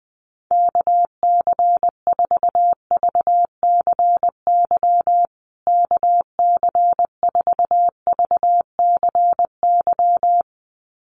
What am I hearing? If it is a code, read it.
Morse code, heard as a keyed tone, about 20 wpm, 700 Hz. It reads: KC4VCY KC4VCY
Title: USA Call Signs Repeated 2x 20wpm